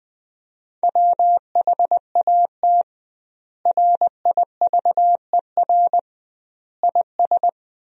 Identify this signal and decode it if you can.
Morse code, heard as a keyed tone, about 21 wpm, 700 Hz. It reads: WHAT RIVER IS
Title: 2x Word-Spacing - Sentences from Top 300 Words - 20wpm